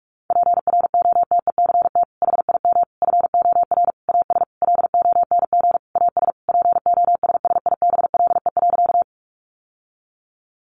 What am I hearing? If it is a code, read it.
Morse code, heard as a keyed tone, about 44 wpm, 700 Hz. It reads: PROTECT HIM FOR AS LONG AS POSSIBLE.
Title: Sentences from Top 900 Words 45wpm